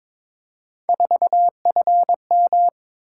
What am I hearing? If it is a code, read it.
Morse code, heard as a keyed tone, about 22 wpm, 700 Hz. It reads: 4FM
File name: practice / 22wpm / Three Letter-Number Combinations - 22wpm